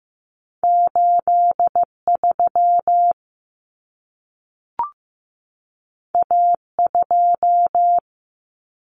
Morse code, heard as a keyed tone, about 15 wpm, 700 Hz. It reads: 83 A2
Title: Common Cut Numbers - 2-Digit 15wpm